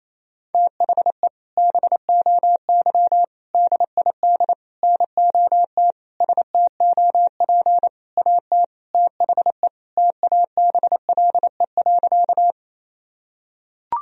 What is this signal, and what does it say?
Morse code, heard as a keyed tone, about 28 wpm, 700 Hz. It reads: THE BOY DID NOT STOP AT THE TABLE.
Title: Sentences from Top 1000 Words - Review Part 2 - 28wpm